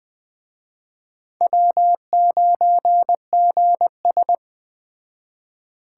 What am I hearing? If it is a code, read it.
Morse code, heard as a keyed tone, about 21 wpm, 700 Hz. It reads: W9GS
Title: Speed Racing - USA 1x2 Call Signs 17wpm